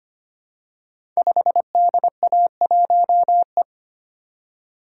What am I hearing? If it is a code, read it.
Morse code, heard as a keyed tone, about 25 wpm, 700 Hz. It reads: 5DA1E
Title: Code Groups - 5 Letter:Numbers - 25wpm